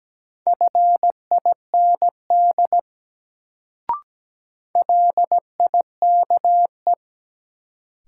Morse code, heard as a keyed tone, about 17 wpm, 700 Hz. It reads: FIND LIKE
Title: Sentences from Top 100 Words - New Word Drill - 17wpm